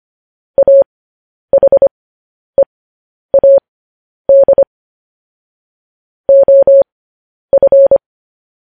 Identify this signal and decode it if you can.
Morse code, heard as a keyed tone, about 25 wpm, spaced as for 10 wpm, 550 Hz. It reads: AHEAD OF
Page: news-headlines-in-morse-code-at-10-wpm